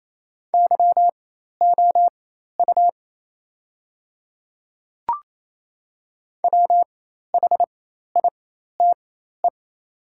Morse code, heard as a keyed tone, about 28 wpm, spaced as for 13 wpm, 700 Hz. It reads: YOU WHITE
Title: 4x Character Spacing - Top 300 Words - 28wpm